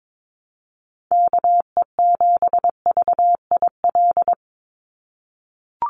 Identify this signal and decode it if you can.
Morse code, heard as a keyed tone, about 22 wpm, 700 Hz. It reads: KE7VIL
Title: USA Call Signs 22wpm